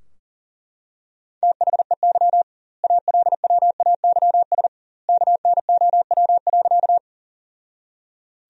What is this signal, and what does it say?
Morse code, heard as a keyed tone, about 40 wpm, 700 Hz. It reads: THEY ALWAYS KNOW.
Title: 2x Word-Spacing - Sentences from Top 300 Words - 40wpm